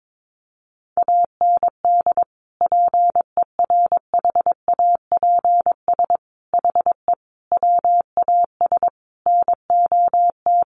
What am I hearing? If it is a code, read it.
Morse code, heard as a keyed tone, about 22 wpm, 700 Hz. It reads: AND PERHAPS HE WAS NOT
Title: Sentences from Top 900 Words 22wpm